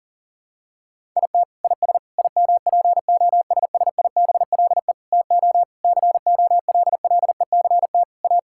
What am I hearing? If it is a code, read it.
Morse code, heard as a keyed tone, about 40 wpm, 700 Hz. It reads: IT IS IMPOSSIBLE TO COLLECT A
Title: Sentences from Top 1600 Words 40wpm